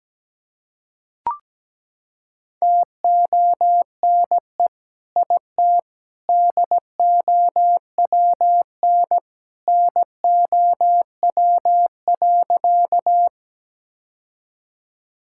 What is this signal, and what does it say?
Morse code, heard as a keyed tone, about 17 wpm, 700 Hz. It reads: TONE IT DOWN NOW.
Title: Sentences from Top 1600 Words 17wpm